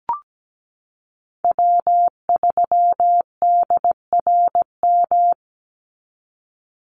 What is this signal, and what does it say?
Morse code, heard as a keyed tone, about 17 wpm, 700 Hz. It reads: W3DRM
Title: USA Call Signs 17wpm